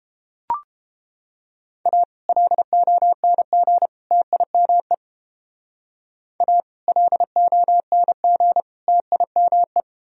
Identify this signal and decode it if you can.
Morse code, heard as a keyed tone, about 29 wpm, 700 Hz. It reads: A LONG TIME A LONG TIME